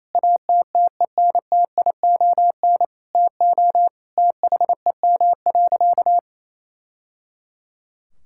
Morse code, heard as a keyed tone, about 28 wpm, 700 Hz. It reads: ATTENTION TO THEM.
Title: Sentences from Top 1000 Words - Review Part 6 - 28wpm